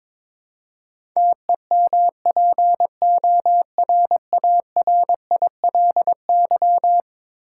Morse code, heard as a keyed tone, about 21 wpm, 700 Hz. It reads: TEMPORARILY
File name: practice / 22wpm / Eleven Letter Words - 22wpm